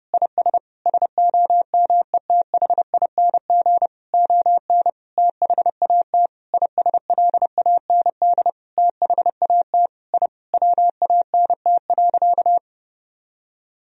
Morse code, heard as a keyed tone, about 30 wpm, 700 Hz. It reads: IS SOMETHING ON THAT ISLAND THAT I WANT.
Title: Sentences from Top 1000 Words - Review Part 3 - 30wpm